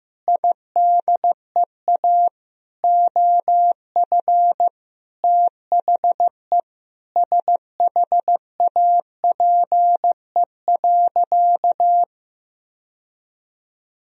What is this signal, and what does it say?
Morse code, heard as a keyed tone, about 15 wpm, 700 Hz. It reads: IDEA OF THE SHAPE.
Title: Sentences from Top 1000 Words - Review Part 3 - 15wpm